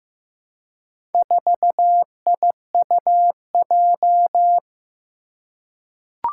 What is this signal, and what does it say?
Morse code, heard as a keyed tone, about 15 wpm, 700 Hz. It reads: 4IUJ
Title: Four Challenging Letter-Number Combinations 15wpm